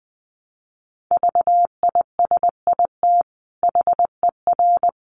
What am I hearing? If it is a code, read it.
Morse code, heard as a keyed tone, about 20 wpm, 700 Hz. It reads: VISIT HER
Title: Sentences from Top 900 Words 20wpm